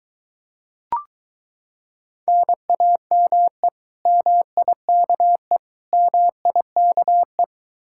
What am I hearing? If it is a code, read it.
Morse code, heard as a keyed tone, about 23 wpm, 700 Hz. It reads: NAME MIKE MIKE